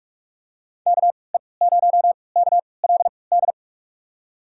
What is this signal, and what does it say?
Morse code, heard as a keyed tone, about 46 wpm, spaced as for 25 wpm, 700 Hz. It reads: KE0KLD
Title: USA Call Signs - 3x Character Spacing - 45wpm